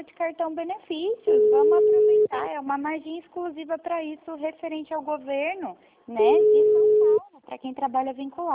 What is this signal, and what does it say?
Ringback tone (Europe, 425 Hz), with speech.